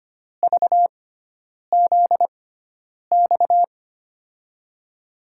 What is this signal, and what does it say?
Morse code, heard as a keyed tone, about 25 wpm, spaced as for 9 wpm, 700 Hz. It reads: VZX
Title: USA Call Signs - 6x Character Spacing - 25wpm